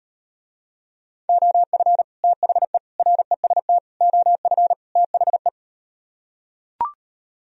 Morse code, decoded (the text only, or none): OF THE REST OF THE